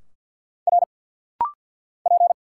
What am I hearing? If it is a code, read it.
Morse code, heard as a keyed tone, about 50 wpm, 690 Hz. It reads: R P
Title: Single Letters - Rapid-Fire 50wpm